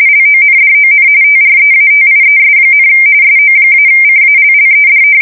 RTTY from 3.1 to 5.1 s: SO-CALLED